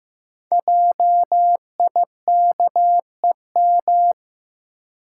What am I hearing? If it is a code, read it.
Morse code, heard as a keyed tone, about 15 wpm, 700 Hz. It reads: JIKEM
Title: Code Groups - 5 Letters - 15wpm